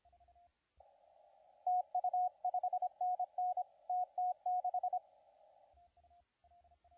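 Morse code, heard as a keyed tone, about 26 wpm, 730 Hz, with radio noise. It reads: TU 5NN TT6
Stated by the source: CW contest audio recorded at LA4C during the SAC CW contest.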